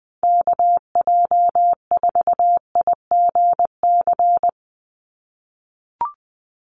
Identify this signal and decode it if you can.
Morse code, heard as a keyed tone, about 20 wpm, 700 Hz. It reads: KJ4IGC
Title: USA Call Signs 20wpm